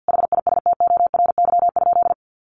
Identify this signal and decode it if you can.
Morse code, heard as a keyed tone, about 50 wpm, 700 Hz. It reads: HISTORY?
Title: Sentences from Top 1100 Words 50wpm